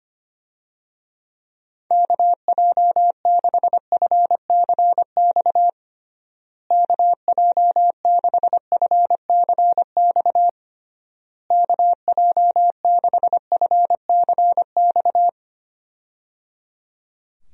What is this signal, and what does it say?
Morse code, heard as a keyed tone, about 25 wpm, 700 Hz. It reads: KJ6FCX KJ6FCX KJ6FCX
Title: USA Call Signs - 6 Characters - Repeated 3x - 25wpm